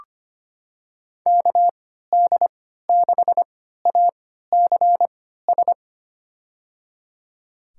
Morse code, heard as a keyed tone, about 25 wpm, spaced as for 14 wpm, 700 Hz. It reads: KD6ACS
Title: USA Call Signs - 6 Characters - 3x Character Spacing - 25wpm